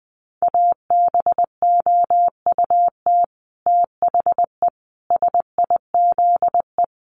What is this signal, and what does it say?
Morse code, heard as a keyed tone, about 20 wpm, 700 Hz. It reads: ABOUT THE SIZE
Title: Sets of 5 Words 20wpm